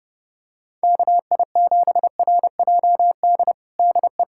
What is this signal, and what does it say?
Morse code, heard as a keyed tone, about 30 wpm, 700 Hz. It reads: KI7RJD DE
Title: USA Callsign DE Callsign - 6 Characters - 30wpm